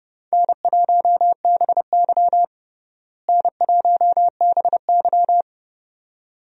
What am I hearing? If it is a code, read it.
Morse code, heard as a keyed tone, about 30 wpm, 700 Hz. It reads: N1BY N1BY
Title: USA Call Signs - 4 Characters - Repeated 2x - 30wpm